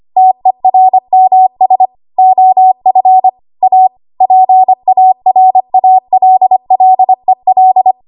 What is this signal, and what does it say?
Morse code, heard as a keyed tone, about 25 wpm, 750 Hz. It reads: TERMS OF A PARALLEL
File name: Archive / 25 WPM / 25WPM